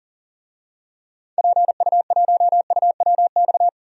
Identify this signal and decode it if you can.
Morse code, heard as a keyed tone, about 40 wpm, 700 Hz. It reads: PU1UWX
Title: DXCC-100 Call Signs - 6 Characters - 40wpm